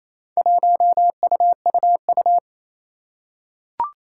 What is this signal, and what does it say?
Morse code, heard as a keyed tone, about 27 wpm, 700 Hz. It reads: 1UUU
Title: Four Challenging Letter-Number Combinations 28wpm